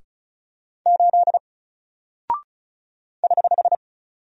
Morse code, heard as a keyed tone, about 36 wpm, 700 Hz. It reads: 8 <HH>
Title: Single Character (Letter:Number:Punctuation:Prosign) 35wpm